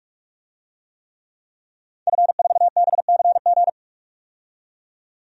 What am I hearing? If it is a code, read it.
Morse code, heard as a keyed tone, about 46 wpm, 700 Hz. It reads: F4BCG
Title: DXCC-100 Call Signs - 5 Characters - 45wpm